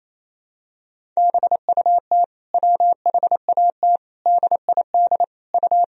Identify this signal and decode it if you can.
Morse code, heard as a keyed tone, about 28 wpm, 700 Hz. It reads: BUT WHAT DID U
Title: Sentences from Top 300 Words and Abbreviations 28wpm